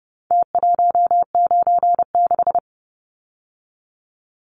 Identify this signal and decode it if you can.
Morse code, heard as a keyed tone, about 30 wpm, 700 Hz. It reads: T196
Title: Numbers - Contest Style - 4-Digit 30wpm